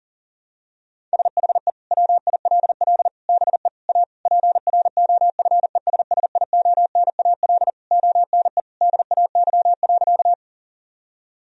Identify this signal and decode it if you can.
Morse code, heard as a keyed tone, about 37 wpm, 700 Hz. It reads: SHE WILL BE A PROFESSIONAL ONE DAY.
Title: Sentences from Top 1000 Words - Review Part 4 - 40wpm